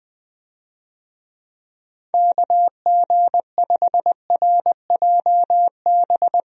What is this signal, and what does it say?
Morse code, heard as a keyed tone, about 20 wpm, 700 Hz. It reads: KG5RJB
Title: USA Call Signs - 6 Characters - Repeated 3x - 20wpm